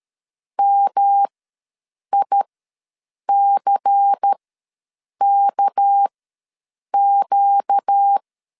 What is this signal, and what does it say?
Morse code, heard as a keyed tone, about 13 wpm, spaced as for 7 wpm, 800 Hz. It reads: MICKQ
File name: LEZIONI / 7 GF9Z / 7 G ws 12wpm 3140